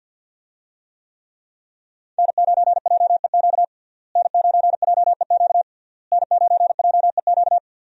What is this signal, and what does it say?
Morse code, heard as a keyed tone, about 49 wpm, 700 Hz. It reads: N9JEQ N9JEQ N9JEQ
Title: USA Call Signs - 5 Characters - Repeated 3x - 50wpm